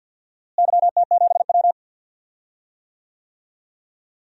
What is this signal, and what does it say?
Morse code, heard as a keyed tone, about 52 wpm, 700 Hz. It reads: YT7W
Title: DXCC-100 Call Signs - 4 Characters - Repeated 3x - 50wpm